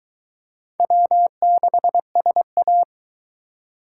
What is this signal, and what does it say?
Morse code, heard as a keyed tone, about 22 wpm, 700 Hz. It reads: W6SA